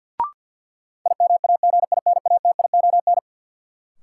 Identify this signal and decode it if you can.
Morse code, heard as a keyed tone, about 53 wpm, 700 Hz. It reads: IMAGINATION
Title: Sentences from Top 1300 Words - New Word Drill Alt - 50wpm